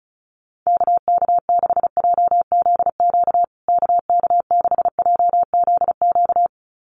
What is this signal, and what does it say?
Morse code, heard as a keyed tone, about 34 wpm, 700 Hz. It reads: KK6JZQ KK6JZQ
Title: USA Call Signs Repeated 2x 35wpm